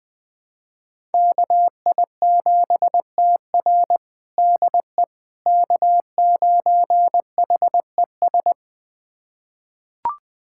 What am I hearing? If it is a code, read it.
Morse code, heard as a keyed tone, about 20 wpm, 700 Hz. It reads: KI7TR DE K9HES
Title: USA Callsign DE Callsign - 5 Characters - 20wpm